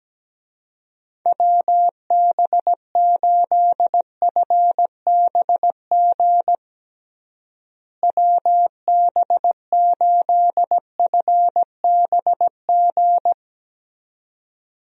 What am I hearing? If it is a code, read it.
Morse code, heard as a keyed tone, about 17 wpm, 700 Hz. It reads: WB8FBG WB8FBG
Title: USA Call Signs - 6 Characters - Repeated 2x - 17wpm